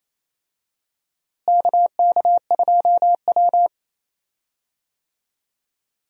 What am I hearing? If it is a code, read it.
Morse code, heard as a keyed tone, about 28 wpm, 700 Hz. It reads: KK2W
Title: USA Call Signs - 4 Characters - Repeated 2x - 28wpm